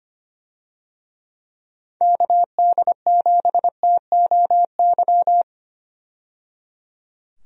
Morse code, heard as a keyed tone, about 25 wpm, 700 Hz. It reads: KD7TOY
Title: USA Call Signs - 6 Characters - 25wpm